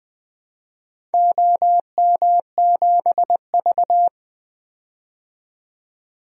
Morse code, heard as a keyed tone, about 21 wpm, 700 Hz. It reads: OM7V
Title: DXCC-100 Call Signs - 4 Characters - 20wpm